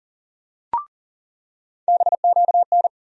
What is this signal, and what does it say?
Morse code, heard as a keyed tone, about 39 wpm, 700 Hz. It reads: BQN
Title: USA Call Sign Postfixes - 3 Letters 40wpm